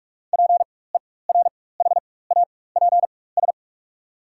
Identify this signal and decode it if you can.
Morse code, heard as a keyed tone, about 44 wpm, spaced as for 21 wpm, 700 Hz. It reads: PERHAPS
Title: 4x Character Spacing - Top 500 Words - 45wpm